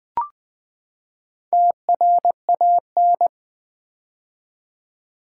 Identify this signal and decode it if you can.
Morse code, heard as a keyed tone, about 20 wpm, 700 Hz. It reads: TRAN
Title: Build A Word - 4 Letters At A Time - 12-Letter Words - 20wpm